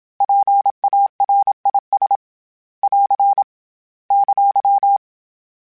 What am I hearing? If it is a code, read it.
Morse code, heard as a keyed tone, about 27 wpm, 800 Hz. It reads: PARIS <AR> !
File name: PARIS Calibration / PARIS mp3s / PARIS@25